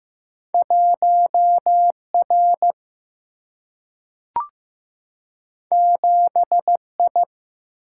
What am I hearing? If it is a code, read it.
Morse code, heard as a keyed tone, about 15 wpm, 700 Hz. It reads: 1 R 7 I
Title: Single-Letter Single-Number 15wpm